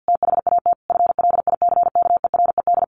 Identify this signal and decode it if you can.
Morse code, heard as a keyed tone, about 50 wpm, 700 Hz. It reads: THAT FLICKERED
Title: A Sound Of Thunder 50wpm